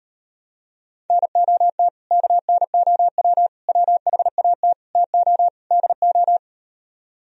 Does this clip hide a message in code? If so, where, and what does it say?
Yes: Morse code from 1.1 to 6.4 s, heard as a keyed tone, about 38 wpm, 700 Hz. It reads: NOT KNOW WHAT TO DO